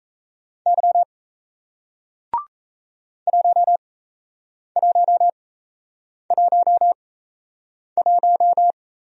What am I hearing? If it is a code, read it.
Morse code, heard as a keyed tone, about 38 wpm, spaced as for 8 wpm, 700 Hz. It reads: Y 1111